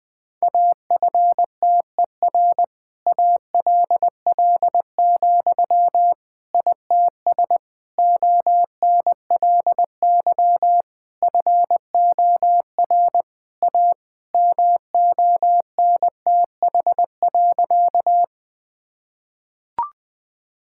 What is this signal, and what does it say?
Morse code, heard as a keyed tone, about 20 wpm, 700 Hz. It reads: AFTER ALL, ITS ONLY FOR A MONTH.
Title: Sentences from Top 1000 Words - Review Part 5 - 20wpm